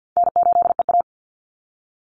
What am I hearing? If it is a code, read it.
Morse code, heard as a keyed tone, about 50 wpm, 700 Hz. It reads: N8EA
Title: Call Signs Repeated 2x - 50wpm